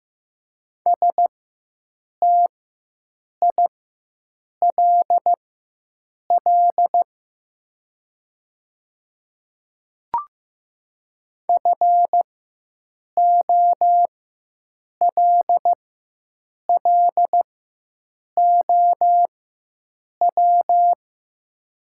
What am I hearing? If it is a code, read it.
Morse code, heard as a keyed tone, about 15 wpm, spaced as for 7 wpm, 700 Hz. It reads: STILL FOLLOW
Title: 4x Character Spacing - Top 300 Words - 15wpm